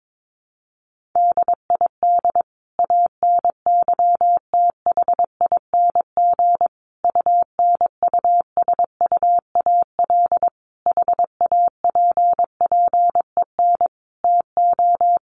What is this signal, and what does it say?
Morse code, heard as a keyed tone, about 22 wpm, 700 Hz. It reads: DID ANYTHING UNUSUAL HAPPEN TO
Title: Sentences from Top 1400 Words 22wpm